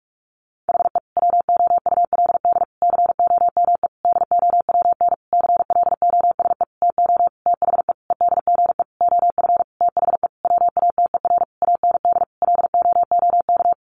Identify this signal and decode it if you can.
Morse code, heard as a keyed tone, about 44 wpm, 700 Hz. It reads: HE WOULD COME DOWN CLOSE TO THE EDGE OF THE WATER AND LOOK
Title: Sentences from Top 1300 Words 45wpm